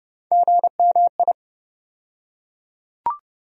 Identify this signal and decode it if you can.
Morse code, heard as a keyed tone, about 30 wpm, 700 Hz. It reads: GMI
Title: USA Call Sign Postfixes - 3 Letters 30wpm